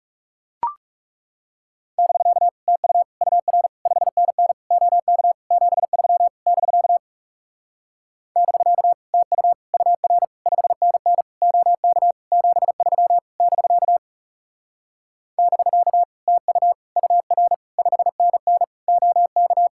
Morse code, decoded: <BK> TU UR 5NN OK 73 <BK> <BK> TU UR 5NN OK 73 <BK> <BK> TU UR 5NN OK

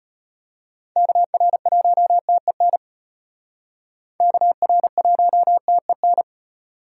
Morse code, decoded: KR1TEN KR1TEN

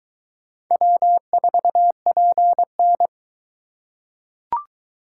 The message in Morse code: W4PN

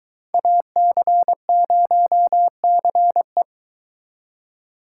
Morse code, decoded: AC0CE

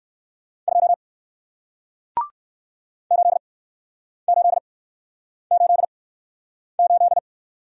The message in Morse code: 2 8888

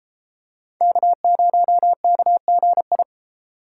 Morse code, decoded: K0KGI

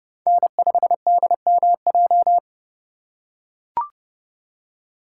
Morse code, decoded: N5DMJ

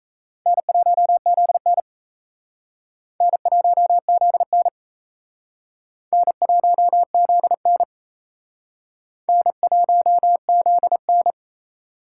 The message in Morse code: N1ZN N1ZN N1ZN N1ZN

